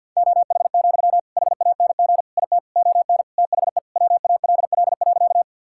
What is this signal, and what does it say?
Morse code, heard as a keyed tone, about 50 wpm, 700 Hz. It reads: OH, HANG IT ON THE WALL.
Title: Sentences from Top 1600 Words 50wpm